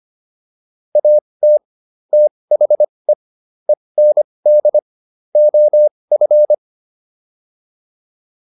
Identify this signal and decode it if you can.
Morse code, heard as a keyed tone, about 25 wpm, spaced as for 20 wpm, 600 Hz. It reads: AT THE END OF